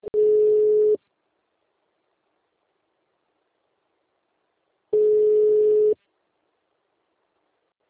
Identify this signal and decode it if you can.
Ringback tone (Europe, 425 Hz).